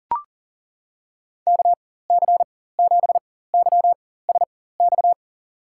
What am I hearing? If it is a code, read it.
Morse code, heard as a keyed tone, about 39 wpm, spaced as for 18 wpm, 700 Hz. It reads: KC7YSX
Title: USA Call Signs - 6 Characters - 4x Character Spacing - 40wpm